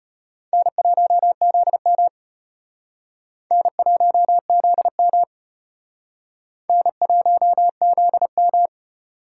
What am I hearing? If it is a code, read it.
Morse code, heard as a keyed tone, about 36 wpm, 700 Hz. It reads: N1ZM N1ZM N1ZM